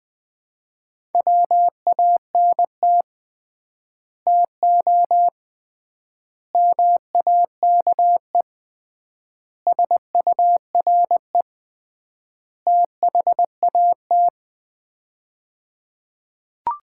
Morse code, heard as a keyed tone, about 20 wpm, 700 Hz. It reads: WANT TO MAKE SURE THAT
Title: 3x Word-Spacing - Sets of 5 Words - 20wpm